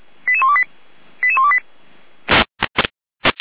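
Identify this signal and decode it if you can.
Five-tone selective call (ZVEI), heard as quick runs of five tones, tones of about 70 ms. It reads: 88128 88128
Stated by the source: SDR-received ZVEI five-tone selective call.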